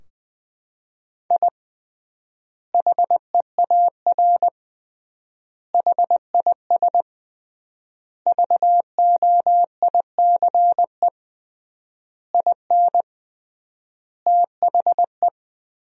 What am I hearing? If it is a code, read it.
Morse code, heard as a keyed tone, about 20 wpm, 700 Hz. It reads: I HEAR HIS VOICE IN THE
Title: 3x Word-Spacing - Sentences from Top 500 Words - 20wpm